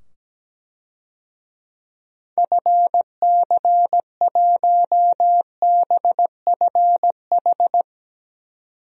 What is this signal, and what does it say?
Morse code, heard as a keyed tone, about 17 wpm, 700 Hz. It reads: FC1BFH
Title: DXCC-100 Call Signs - 6 Characters - 17wpm